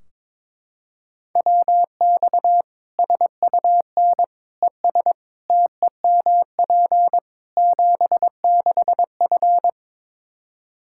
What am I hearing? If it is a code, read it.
Morse code, heard as a keyed tone, about 22 wpm, 700 Hz. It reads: WX SUN ES TEMP 76F
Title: Practice QSO - Exchange 2 - Info - 22wpm